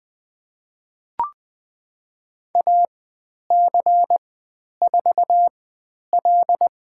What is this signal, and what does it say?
Morse code, heard as a keyed tone, about 20 wpm, spaced as for 10 wpm, 700 Hz. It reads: AC4L
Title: 095 Lesson 16g - Letter c - Call Signs 20wpm